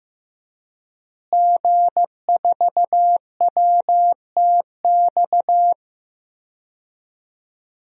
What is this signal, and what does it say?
Morse code, heard as a keyed tone, about 15 wpm, 700 Hz. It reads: G4WTX
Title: DXCC-100 Call Signs - 5 Characters - Repeated 3x - 15wpm